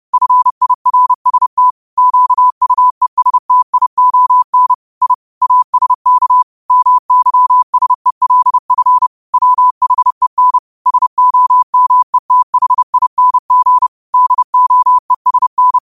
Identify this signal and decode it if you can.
Morse code, heard as a keyed tone, about 30 wpm, 1000 Hz. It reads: FIRST QUESTION I ASK MYSELF WHEN SOMETHING DOESN
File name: qotdfiles / 30wpm